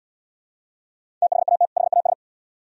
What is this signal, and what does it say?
Morse code, heard as a keyed tone, about 68 wpm, 700 Hz. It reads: THAT HAS